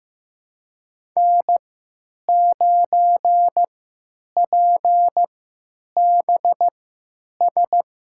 Morse code, heard as a keyed tone, about 16 wpm, spaced as for 9 wpm, 700 Hz. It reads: N9PBS
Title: USA Call Signs - 5 Characters - 3x Character Spacing - 15wpm